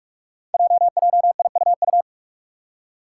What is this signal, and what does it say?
Morse code, heard as a keyed tone, about 42 wpm, 700 Hz. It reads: JJIUU
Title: Five Challenging Letter-Number Combinations 45wpm